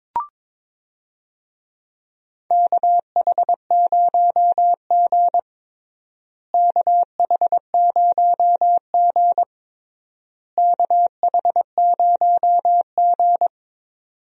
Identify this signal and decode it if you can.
Morse code, heard as a keyed tone, about 22 wpm, 700 Hz. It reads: KH0G KH0G KH0G
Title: USA Call Signs - 4 Characters - Repeated 3x - 22wpm